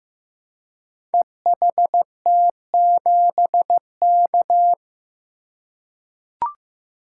Morse code, heard as a keyed tone, about 15 wpm, 700 Hz. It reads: EHT7K
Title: Code Groups - 5 Letter:Numbers - 15wpm